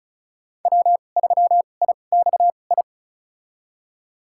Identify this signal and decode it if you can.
Morse code, heard as a keyed tone, about 35 wpm, spaced as for 25 wpm, 700 Hz. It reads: W3IXI
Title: USA Call Signs - 5 Characters - 2x Character Spacing - 35wpm